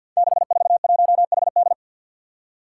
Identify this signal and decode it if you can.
Morse code, heard as a keyed tone, about 53 wpm, 700 Hz. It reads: 641HD
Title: Five Challenging Letter-Number Combinations 50wpm